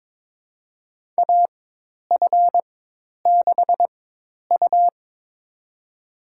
Morse code, heard as a keyed tone, about 22 wpm, spaced as for 10 wpm, 700 Hz. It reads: AF6U
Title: USA Call Signs - 4 Characters - 4x Character Spacing - 22wpm